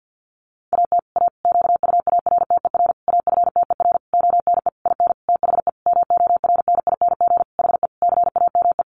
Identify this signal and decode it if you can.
Morse code, heard as a keyed tone, about 50 wpm, 700 Hz. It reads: AT A QUARTER AFTER ONE IN THE MORNING HE CAME
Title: Sentences from Top 1300 Words 50wpm